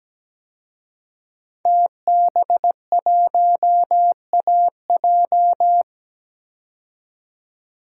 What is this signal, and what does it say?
Morse code, heard as a keyed tone, about 17 wpm, 700 Hz. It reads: TB1AJ
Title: DXCC-100 Call Signs - 5 Characters - 17wpm